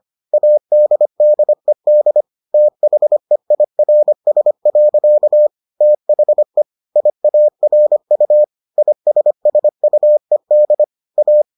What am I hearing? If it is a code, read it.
Morse code, heard as a keyed tone, about 25 wpm, 600 Hz. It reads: ADDED THEIRS. THE IARU ISSUED A